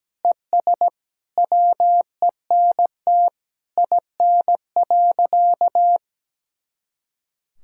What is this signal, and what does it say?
Morse code, heard as a keyed tone, about 17 wpm, 700 Hz. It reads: ES WENT IN.
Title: Sentences from Top 300 Words and Abbreviations 17wpm